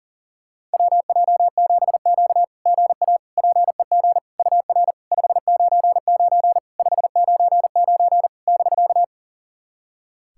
Morse code, heard as a keyed tone, about 40 wpm, 690 Hz. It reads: WJ7Q GA PEG UR 599 599 <BK>
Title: SOTA QSO Practice 40wpm